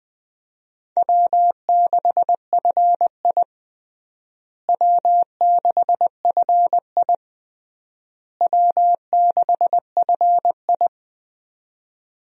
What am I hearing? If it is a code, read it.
Morse code, heard as a keyed tone, about 21 wpm, 700 Hz. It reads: W6FI W6FI W6FI
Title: USA Call Signs - 4 Characters - Repeated 3x - 20wpm